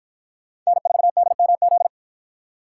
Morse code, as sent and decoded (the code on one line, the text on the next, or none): -. ....- -.. -- --..
N4DMZ